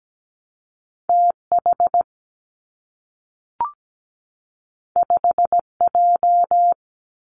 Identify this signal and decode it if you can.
Morse code, heard as a keyed tone, about 17 wpm, 700 Hz. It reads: T H 5 J
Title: Two Challenging Letter-Number Combinations 17wpm